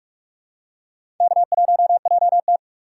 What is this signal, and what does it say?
Morse code, heard as a keyed tone, about 46 wpm, 700 Hz. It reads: K1JT
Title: USA Call Signs - 2x Character Spacing - 45wpm